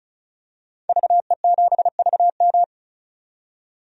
Morse code, heard as a keyed tone, about 36 wpm, 700 Hz. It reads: VE7VM
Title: DXCC-100 Call Signs - 5 Characters - 35wpm